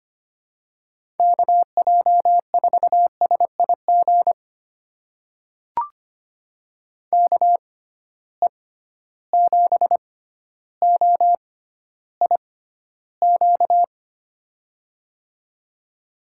Morse code, heard as a keyed tone, about 25 wpm, spaced as for 13 wpm, 700 Hz. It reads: KJ4SIG KE7OIQ
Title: USA Call Signs - 6 Characters - 6x Character Spacing - 25wpm